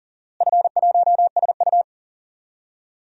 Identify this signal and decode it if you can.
Morse code, heard as a keyed tone, about 40 wpm, 700 Hz. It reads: F1SU
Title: DXCC-100 Call Signs - 4 Characters - Repeated 2x - 40wpm